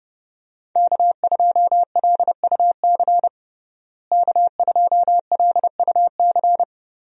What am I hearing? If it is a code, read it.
Morse code, heard as a keyed tone, about 30 wpm, 700 Hz. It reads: K2LUC K2LUC
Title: USA Call Signs - 5 Characters - Repeated 3x - 30wpm